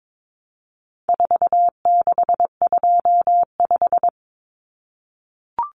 Morse code, heard as a keyed tone, about 22 wpm, 700 Hz. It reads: 4625
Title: Common Cut Numbers - 4-Digit 22wpm